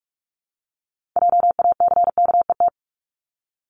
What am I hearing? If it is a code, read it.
Morse code, heard as a keyed tone, about 46 wpm, 700 Hz. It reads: JACKET
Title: Multi-letter Phonemes 45wpm